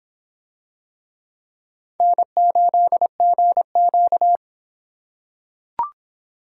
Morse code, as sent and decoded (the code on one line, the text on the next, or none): -. ---.. --. --.-
N8GQ